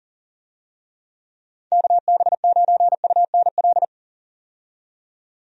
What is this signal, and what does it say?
Morse code, heard as a keyed tone, about 40 wpm, 700 Hz. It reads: KB9UNL
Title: USA Call Signs - 6 Characters - 40wpm